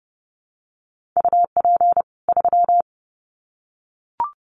Morse code, heard as a keyed tone, about 30 wpm, 700 Hz. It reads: UP 3
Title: Common QSO Phrases 30wpm